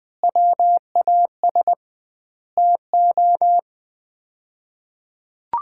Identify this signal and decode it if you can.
Morse code, heard as a keyed tone, about 21 wpm, 700 Hz. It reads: WAS TO
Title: 2x Word-Spacing - Sets of 5 Words - 20wpm